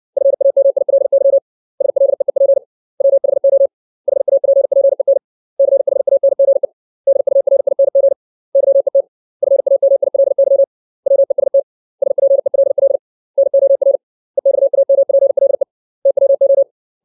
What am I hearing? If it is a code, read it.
Morse code, heard as a keyed tone, about 60 wpm, 550 Hz. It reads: FAMILY SLEEP WHO HAPPEN CHANGE DURING YET FAMILY WEST SPELL NOR EXAMPLE TWO
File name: 632 words / 632 Words 60WPM